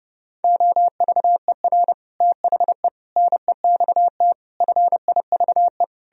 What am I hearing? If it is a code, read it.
Morse code, heard as a keyed tone, about 40 wpm, spaced as for 35 wpm, 700 Hz. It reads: OVER THE NEXT FIVE
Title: Speed-Racing - Sets of 5 Words 30wpm